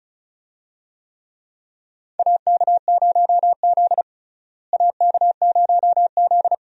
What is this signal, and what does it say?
Morse code, heard as a keyed tone, about 36 wpm, 700 Hz. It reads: AK0Z AK0Z
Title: USA Call Signs - 4 Characters - Repeated 2x - 35wpm